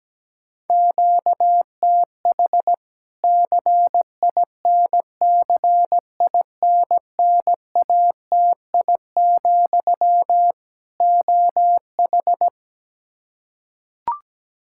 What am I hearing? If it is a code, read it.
Morse code, heard as a keyed tone, about 17 wpm, 700 Hz. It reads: QTH CINCINNATI, OH
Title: Practice QSO - Exchange 1 - Beginning - 17wpm